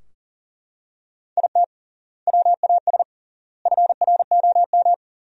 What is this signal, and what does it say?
Morse code, heard as a keyed tone, about 40 wpm, 700 Hz. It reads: IT WAS FROM
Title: 3x Word-Spacing - Sentences from Top 100 Words - 40wpm